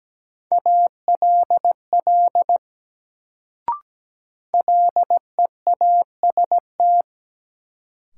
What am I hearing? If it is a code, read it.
Morse code, heard as a keyed tone, about 17 wpm, 700 Hz. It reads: ALL LEAST
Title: Sets of 2 Words - New Word Drill - 17wpm